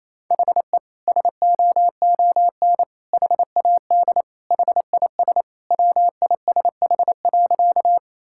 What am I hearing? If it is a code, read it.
Morse code, heard as a keyed tone, about 28 wpm, 700 Hz. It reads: HE SOON HAD HIS WISH.
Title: Sentences from Top 1600 Words 28wpm